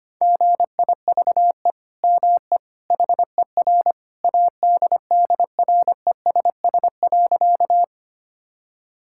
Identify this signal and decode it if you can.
Morse code, heard as a keyed tone, about 25 wpm, 700 Hz. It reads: GIVE ME HER ADDRESS.
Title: Sentences from Top 1000 Words - Review Part 6 - 25wpm